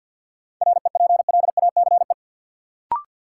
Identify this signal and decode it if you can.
Morse code, heard as a keyed tone, about 50 wpm, 700 Hz. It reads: REPLACE
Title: Sentences from Top 1400 Words - New Word Drill Alt - 50wpm